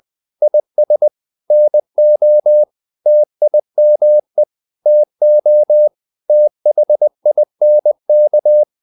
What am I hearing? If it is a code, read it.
Morse code, heard as a keyed tone, about 20 wpm, 600 Hz. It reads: IS NO TIME TO THINK
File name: the old man and the sea part 2 20wpm